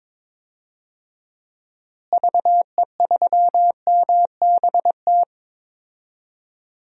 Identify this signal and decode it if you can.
Morse code, heard as a keyed tone, about 22 wpm, 700 Hz. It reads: VE3MBT
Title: DXCC-100 Call Signs - 6 Characters - Repeated 3x - 22wpm